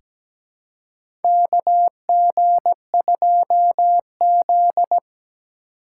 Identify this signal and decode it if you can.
Morse code, heard as a keyed tone, about 17 wpm, 700 Hz. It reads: KG2Z
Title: USA Call Signs - 4 Characters - 2x Character Spacing - 17wpm